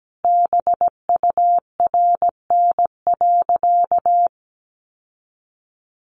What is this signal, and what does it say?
Morse code, heard as a keyed tone, about 17 wpm, 700 Hz. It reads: BURN.
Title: Sentences from Top 1300 Words 17wpm